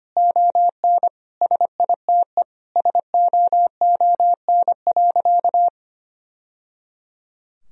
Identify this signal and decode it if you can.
Morse code, heard as a keyed tone, about 25 wpm, 700 Hz. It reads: ON SITE SOON.
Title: Sentences from Top 1000 Words - Review Part 4 - 25wpm